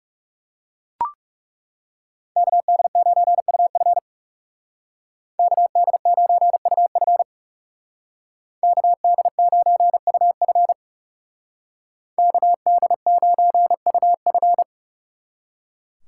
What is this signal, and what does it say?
Morse code, heard as a keyed tone, about 39 wpm, 700 Hz. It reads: KD9UF KD9UF KD9UF KD9UF